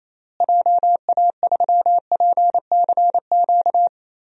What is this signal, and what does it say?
Morse code, heard as a keyed tone, about 28 wpm, 700 Hz. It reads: JA3PCQ
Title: DXCC-100 Call Signs - 6 Characters - Repeated 3x - 28wpm